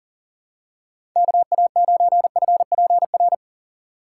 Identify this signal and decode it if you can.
Morse code, heard as a keyed tone, about 40 wpm, 700 Hz. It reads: KA9FPR
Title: USA Call Signs - 6 Characters - 6x Character Spacing - 40wpm